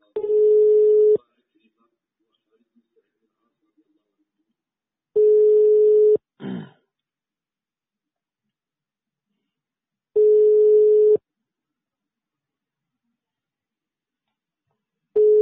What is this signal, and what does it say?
Ringback tone (Europe, 425 Hz), with speech.